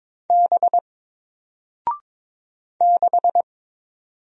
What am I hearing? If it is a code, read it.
Morse code, heard as a keyed tone, about 22 wpm, 700 Hz. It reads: B 6
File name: Single-Letter-Number-22wpm